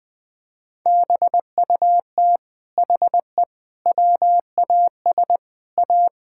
Morse code, heard as a keyed tone, about 21 wpm, 700 Hz. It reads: BUT HE WAS A
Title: Sentences from Top 1000 Words - Review Part 6 - 20wpm